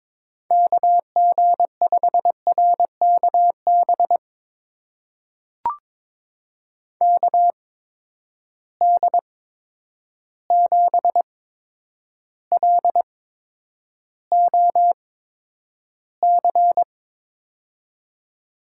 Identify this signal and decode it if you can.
Morse code, heard as a keyed tone, about 22 wpm, spaced as for 9 wpm, 700 Hz. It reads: KG5RKB KD7LOC
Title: USA Call Signs - 6 Characters - 8x Character Spacing - 22wpm